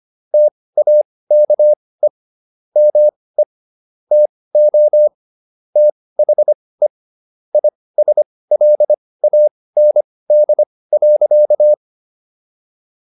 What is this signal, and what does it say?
Morse code, heard as a keyed tone, about 25 wpm, spaced as for 18 wpm, 600 Hz. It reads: TAKE ME TO THE ISLAND.